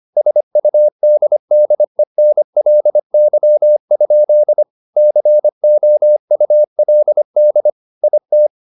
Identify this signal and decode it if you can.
Morse code, heard as a keyed tone, about 25 wpm, 600 Hz. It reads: SUDDENLY? COULD IT